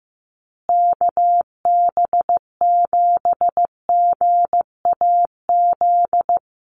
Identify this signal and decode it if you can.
Morse code, heard as a keyed tone, about 15 wpm, 700 Hz. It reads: KB7GAZ
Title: USA Call Signs Repeated 2x 15wpm